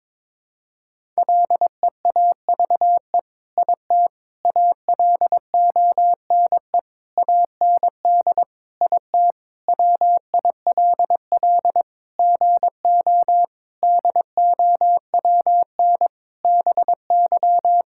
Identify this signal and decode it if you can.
Morse code, heard as a keyed tone, about 22 wpm, 700 Hz. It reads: LEAVE IT ALONE AND IT WILL GO DOWN BY
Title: Sentences from Top 1000 Words - Review Part 5 - 22wpm